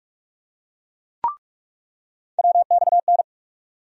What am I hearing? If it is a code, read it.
Morse code, heard as a keyed tone, about 47 wpm, 700 Hz. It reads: WXN
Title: USA Call Sign Postfixes - 3 Letters 45wpm